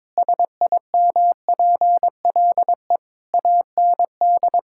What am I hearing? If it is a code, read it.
Morse code, heard as a keyed tone, about 22 wpm, 700 Hz. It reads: SIMPLE AND
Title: Sentences from Top 1600 Words 22wpm